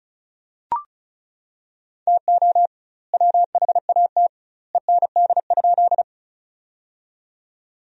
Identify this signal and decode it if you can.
Morse code, heard as a keyed tone, about 34 wpm, 700 Hz. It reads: TO WHAT END?
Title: 2x Word-Spacing - Sentences from Top 200 Words - 35wpm